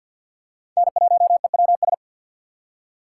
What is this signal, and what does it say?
Morse code, heard as a keyed tone, about 47 wpm, 700 Hz. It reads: N1EWS
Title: USA Call Signs - 5 Characters - 6x Character Spacing - 50wpm